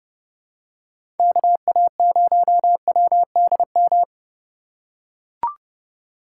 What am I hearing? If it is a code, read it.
Morse code, heard as a keyed tone, about 31 wpm, 700 Hz. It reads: KA0WDM